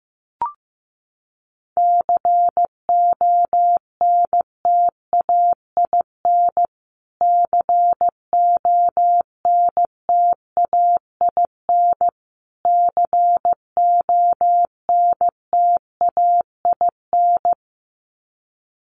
Morse code, heard as a keyed tone, about 15 wpm, 700 Hz. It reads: CONTAIN CONTAIN CONTAIN
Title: Top 500 Words Repeated 15wpm